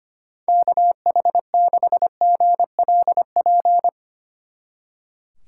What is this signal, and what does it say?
Morse code, heard as a keyed tone, about 31 wpm, 700 Hz. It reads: KH6GLP
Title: Speed Racing - USA 2x3 Call Signs 25wpm